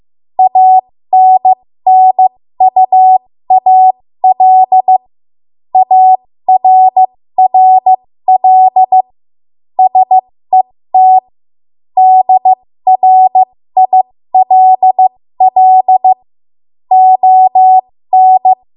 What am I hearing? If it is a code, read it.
Morse code, heard as a keyed tone, about 15 wpm, spaced as for 13 wpm, 750 Hz. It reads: ANNUAL ARRL SET DRILL ON
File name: Morse / 13WPM